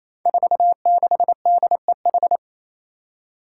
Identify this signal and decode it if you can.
Morse code, heard as a keyed tone, about 29 wpm, 700 Hz. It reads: 46DEH
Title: Five Challenging Letter-Number Combinations 28wpm